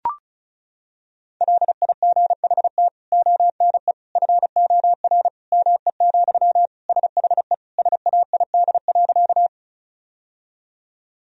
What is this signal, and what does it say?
Morse code, heard as a keyed tone, about 34 wpm, 700 Hz. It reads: LIGHT ONE FOR ME, SHE SAID.
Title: Sentences from Top 1000 Words - Review Part 1 - 35wpm